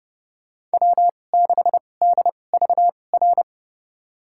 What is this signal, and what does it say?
Morse code, heard as a keyed tone, about 30 wpm, spaced as for 22 wpm, 700 Hz. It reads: W6DVR
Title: USA Call Signs - 5 Characters - 2x Character Spacing - 30wpm